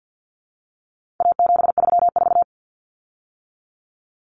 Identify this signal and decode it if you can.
Morse code, heard as a keyed tone, about 50 wpm, 700 Hz. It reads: A734
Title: Common Cut Numbers - 4-Digit 50wpm